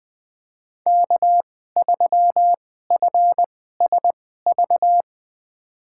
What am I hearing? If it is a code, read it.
Morse code, heard as a keyed tone, about 21 wpm, spaced as for 15 wpm, 700 Hz. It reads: K3FSV
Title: USA Call Signs - 5 Characters - 2x Character Spacing - 20wpm